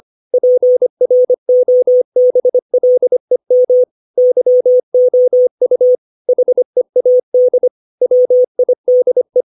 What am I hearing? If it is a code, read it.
Morse code, heard as a keyed tone, about 25 wpm, 500 Hz. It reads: PROBLEM YOU HEAD WIDE
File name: 632 words / 632 Words 25WPM